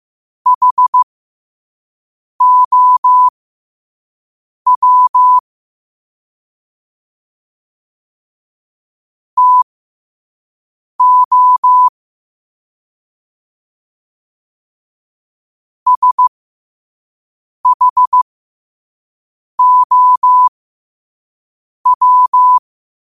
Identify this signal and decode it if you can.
Morse code, heard as a keyed tone, about 15 wpm, spaced as for 5 wpm, 1000 Hz. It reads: HOW TO SHOW
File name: qotdfiles / 5wpm